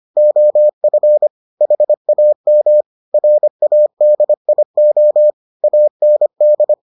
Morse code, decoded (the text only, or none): OF HAM RADIO AND